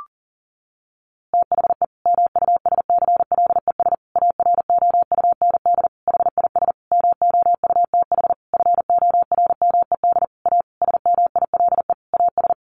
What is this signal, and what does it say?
Morse code, heard as a keyed tone, about 40 wpm, 700 Hz. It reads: THE MUSCLES AROUND HIS MOUTH FORMED A SMILE AS
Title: A Sound Of Thunder 40wpm